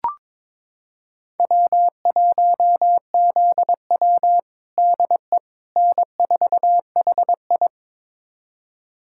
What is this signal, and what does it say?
Morse code, heard as a keyed tone, about 21 wpm, 700 Hz. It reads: W1ZW DE N4HI
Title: USA Callsign DE Callsign - 4 Characters - 22wpm